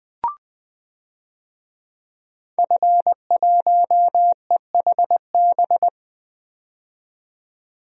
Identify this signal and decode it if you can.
Morse code, heard as a keyed tone, about 20 wpm, 700 Hz. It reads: F1EHB
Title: DXCC-100 Call Signs - 5 Characters - 20wpm